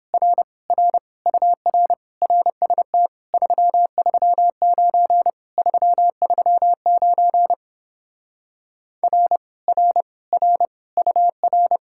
Morse code, heard as a keyed tone, about 31 wpm, 700 Hz. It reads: R R UR RST 339 339 R R R UR